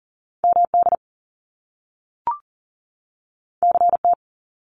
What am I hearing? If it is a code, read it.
Morse code, heard as a keyed tone, about 39 wpm, 700 Hz. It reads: MD CT
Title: US States 40wpm